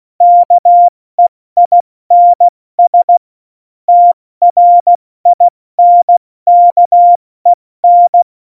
Morse code, heard as a keyed tone, about 16 wpm, 700 Hz. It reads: KEINS TRINKEN
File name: Morsekurs / Witze / Witz 08. Oktober 2019, 70 F80